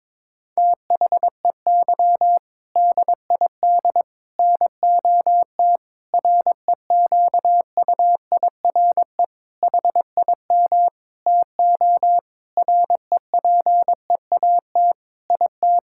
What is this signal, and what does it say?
Morse code, heard as a keyed tone, about 22 wpm, 700 Hz. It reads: THEY DID NOT REQUIRE HIM TO REPEAT IT
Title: Sentences from Top 1600 Words 22wpm